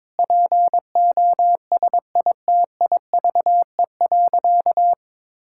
Morse code, heard as a keyed tone, about 22 wpm, 700 Hz. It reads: POSITIVE.
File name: practice / 22wpm / Sentences with 8-Word Phrases - 22wpm